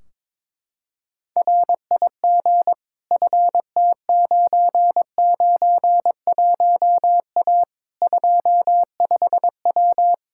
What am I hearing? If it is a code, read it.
Morse code, heard as a keyed tone, about 22 wpm, 700 Hz. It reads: RIG FT991A 25W
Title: Practice QSO - Exchange 2 - Info - 22wpm